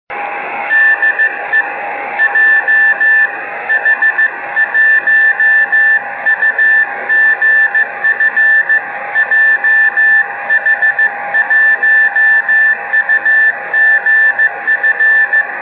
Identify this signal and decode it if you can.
Morse code, heard as a keyed tone, about 15 wpm, 1750 Hz, with noise. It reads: DE JH1UGF JH1UGF
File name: Beacon / 5GbeaconJH1UGF awasaki